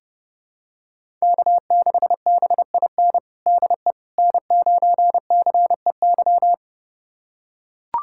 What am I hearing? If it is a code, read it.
Morse code, heard as a keyed tone, about 30 wpm, 700 Hz. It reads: K6BIN DE N9CEY
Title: USA Callsign DE Callsign - 5 Characters - 30wpm